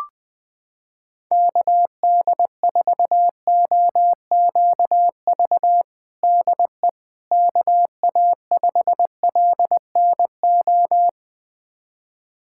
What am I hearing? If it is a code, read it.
Morse code, heard as a keyed tone, about 21 wpm, 700 Hz. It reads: KD4OQV DE KA5LNO
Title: USA Callsign DE Callsign - 6 Characters - 20wpm